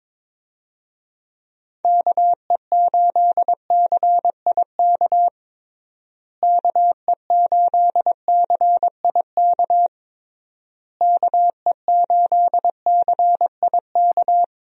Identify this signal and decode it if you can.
Morse code, heard as a keyed tone, about 22 wpm, 700 Hz. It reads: KE8CIK KE8CIK KE8CIK
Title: USA Call Signs - 6 Characters - Repeated 3x - 22wpm